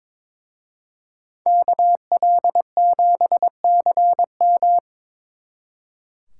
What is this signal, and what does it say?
Morse code, heard as a keyed tone, about 26 wpm, 700 Hz. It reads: KL7CM
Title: Speed Racing - USA 2x2 Call Signs 22wpm